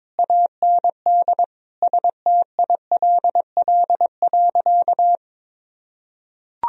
Morse code, heard as a keyed tone, about 22 wpm, 700 Hz. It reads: AND STILL.
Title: Sentences from Top 1000 Words - Review Part 2 - 22wpm